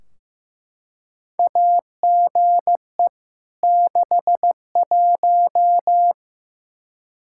Morse code, heard as a keyed tone, about 15 wpm, 700 Hz. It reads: AGE 61
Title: K1USN SST QSO Practice 15wpm